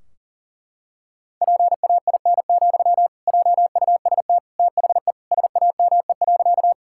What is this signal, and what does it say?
Morse code, heard as a keyed tone, about 40 wpm, 700 Hz. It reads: PAIN, JUST THE SAME.
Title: Sentences from Top 1600 Words 40wpm